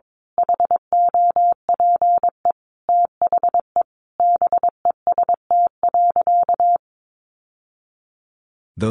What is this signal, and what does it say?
Morse code, heard as a keyed tone, about 22 wpm, 700 Hz. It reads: HOPE THE BEST.
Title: Sentences from Top 1000 Words 22wpm